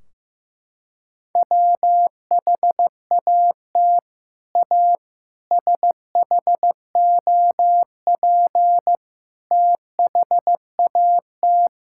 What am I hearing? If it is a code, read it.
Morse code, heard as a keyed tone, about 15 wpm, 700 Hz. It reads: WHAT A SHOP THAT
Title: Sentences from Top 1600 Words 15wpm